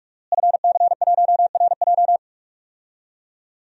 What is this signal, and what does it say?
Morse code, heard as a keyed tone, about 45 wpm, 700 Hz. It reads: FC1RJ
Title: DXCC-100 Call Signs - 5 Characters - Repeated 2x - 45wpm